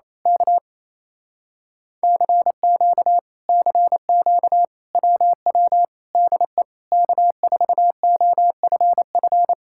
Morse code, heard as a keyed tone, about 26 wpm, 700 Hz. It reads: K CQ CQ WW DE K4OFF
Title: Speed-Racing Common QSO Phrases Encore - 28wpm